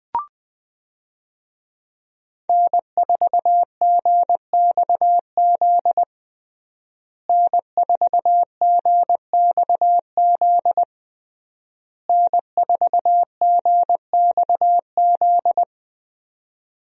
Morse code, heard as a keyed tone, about 20 wpm, 700 Hz. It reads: N4GXZ N4GXZ N4GXZ
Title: USA Call Signs - 5 Characters - Repeated 3x - 20wpm